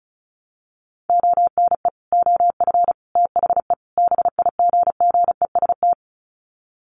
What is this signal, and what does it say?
Morse code, heard as a keyed tone, about 34 wpm, 700 Hz. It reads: ONE OF THE BIGGEST
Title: Sets of 4 Words 35wpm